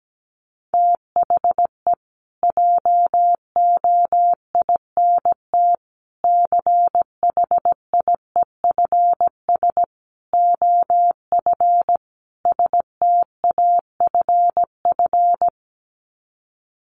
Morse code, heard as a keyed tone, about 17 wpm, 700 Hz. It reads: THE JOINT CHIEFS OF STAFF
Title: Sets of 5 Words Encore2 17wpm